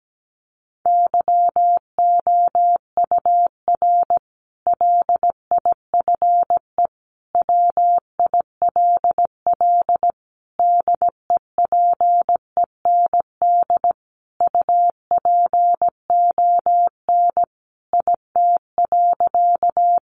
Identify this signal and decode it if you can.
Morse code, heard as a keyed tone, about 17 wpm, 700 Hz. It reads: YOUR LIFE WILL DEPEND UPON IT.
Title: Sentences from Top 1300 Words 17wpm